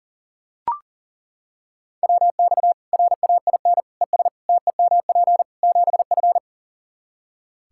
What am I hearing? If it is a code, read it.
Morse code, heard as a keyed tone, about 40 wpm, 700 Hz. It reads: WX RAIN ES TEMP 7F
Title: Practice QSO - Exchange 2 - Info - 40wpm